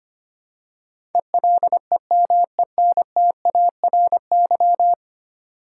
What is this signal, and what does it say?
Morse code, heard as a keyed tone, about 25 wpm, 700 Hz. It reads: ELEMENTARY
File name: 10 Letter Words - First 3 Letters Given - 25wpm